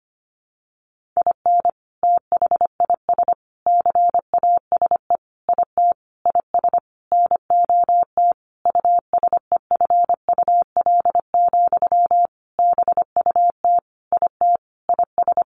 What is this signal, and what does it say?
Morse code, heard as a keyed tone, about 25 wpm, 700 Hz. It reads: IN THIS CASE IT IS NOT USEFUL, BUT IT IS
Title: Sentences from Top 1400 Words 25wpm